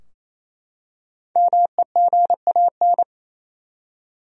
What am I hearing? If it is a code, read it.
Morse code, heard as a keyed tone, about 28 wpm, 700 Hz. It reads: MEGAN
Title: USA Common Names - Max 5 Letters - 28wpm